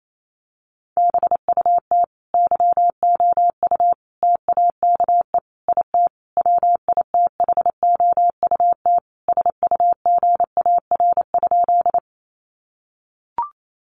Morse code, heard as a keyed tone, about 28 wpm, 700 Hz. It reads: BUT YOU TAKE IT WITHOUT SUGAR?
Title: Sentences from Top 1200 Words 28wpm